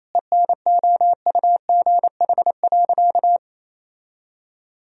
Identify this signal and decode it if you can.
Morse code, heard as a keyed tone, about 28 wpm, 700 Hz. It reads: ENOUGH.
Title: Sentences from Top 1000 Words - Review Part 2 - 28wpm